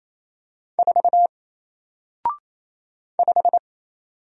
Morse code, heard as a keyed tone, about 27 wpm, 700 Hz. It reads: 4 5
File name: Single-Letter-Number-28wpm